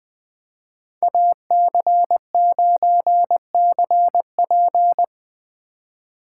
Morse code, heard as a keyed tone, about 20 wpm, 700 Hz. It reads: AC9CP
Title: Speed Racing - USA Call Signs 17wpm